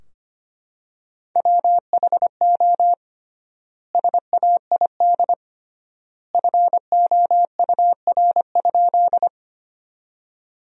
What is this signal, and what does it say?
Morse code, heard as a keyed tone, about 25 wpm, 700 Hz. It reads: WHO SAID FOUR?
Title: 3x Word-Spacing - Sentences from Top 300 Words - 25wpm